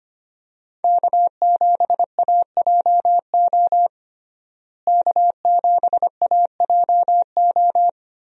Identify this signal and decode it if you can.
Morse code, heard as a keyed tone, about 25 wpm, 700 Hz. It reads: K7AJO K7AJO
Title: USA Call Signs - 5 Characters - Repeated 3x - 25wpm